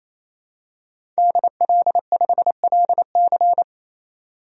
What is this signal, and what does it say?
Morse code, heard as a keyed tone, about 28 wpm, 700 Hz. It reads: DL5LC
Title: DXCC-100 Call Signs - 5 Characters - 28wpm